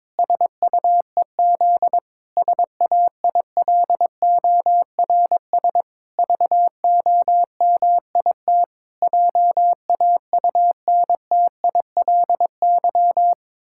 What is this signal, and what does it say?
Morse code, heard as a keyed tone, about 22 wpm, 700 Hz. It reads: SUEZ SAILORS VOMIT JAUNTILY
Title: Pangram Challenge v2 - 22wpm